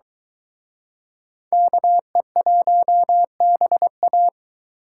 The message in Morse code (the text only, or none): KE1BA